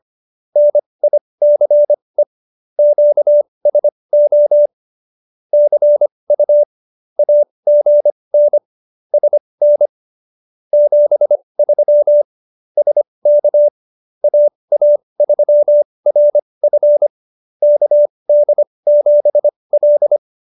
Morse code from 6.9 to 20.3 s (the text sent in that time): AGN SN 73 SK AA3RF KD7L